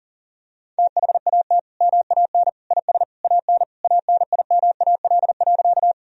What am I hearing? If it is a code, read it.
Morse code, heard as a keyed tone, about 37 wpm, 700 Hz. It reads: THAT MAN IS AN ANIMAL.
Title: Sentences from Top 1000 Words - Review Part 1 - 40wpm